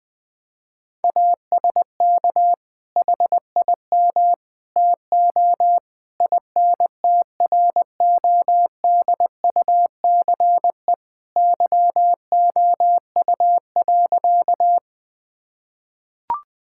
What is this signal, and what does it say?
Morse code, heard as a keyed tone, about 20 wpm, 700 Hz. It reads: ASK HIM TO INTRODUCE YOU.
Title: Sentences from Top 1600 Words 20wpm